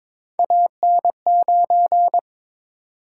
Morse code, heard as a keyed tone, about 22 wpm, 700 Hz. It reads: AN9
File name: practice / 22wpm / Three Letter-Number Combinations - 22wpm